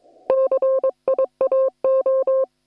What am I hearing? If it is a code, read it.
Morse code, heard as a keyed tone, about 22 wpm, 530 Hz. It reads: CIAO
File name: attachments / hf 1 001